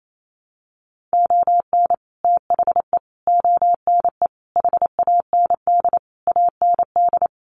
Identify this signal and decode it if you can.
Morse code, heard as a keyed tone, about 28 wpm, 700 Hz. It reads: ON THE ONE HAND AND
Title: Sets of 5 Words Encore2 28wpm